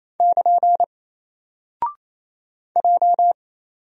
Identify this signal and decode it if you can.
Morse code, heard as a keyed tone, about 27 wpm, 700 Hz. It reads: <KN> J
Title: Single Character (Letter:Number:Punctuation:Prosign) 28wpm